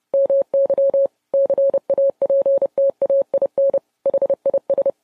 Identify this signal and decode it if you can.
Morse code, heard as a keyed tone, about 30 wpm, 550 Hz. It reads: MY CAPTAIN HIS